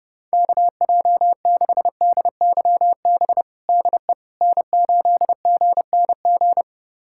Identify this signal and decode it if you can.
Morse code, heard as a keyed tone, about 31 wpm, 700 Hz. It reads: KJ6DYB DE N8GNG
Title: USA Callsign DE Callsign 30wpm